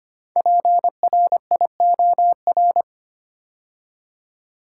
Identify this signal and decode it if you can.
Morse code, heard as a keyed tone, about 25 wpm, 700 Hz. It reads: PRIOR
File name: 5 Letter Words - First 3 Letters Given - 25wpm